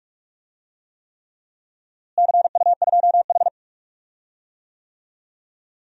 Morse code, heard as a keyed tone, about 46 wpm, 700 Hz. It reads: CU2H
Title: DXCC-100 Call Signs - 4 Characters - Repeated 3x - 45wpm